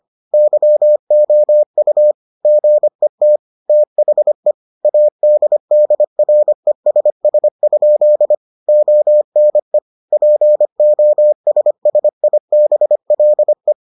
Morse code, heard as a keyed tone, about 25 wpm, 600 Hz. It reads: YOU GET THE ADDRESS? ONE POSSIBLE